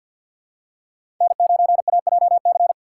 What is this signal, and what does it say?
Morse code, heard as a keyed tone, about 47 wpm, 700 Hz. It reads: N9AJC
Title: USA Call Signs - 5 Characters - 3x Character Spacing - 50wpm